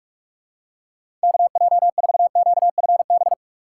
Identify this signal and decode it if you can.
Morse code, heard as a keyed tone, about 46 wpm, 700 Hz. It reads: KJ4QFB
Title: USA Call Signs - 6 Characters - 3x Character Spacing - 45wpm